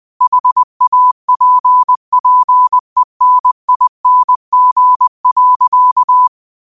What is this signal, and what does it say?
Morse code, heard as a keyed tone, about 20 wpm, 1000 Hz. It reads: HAPPENING.
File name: qotdfiles / 20wpm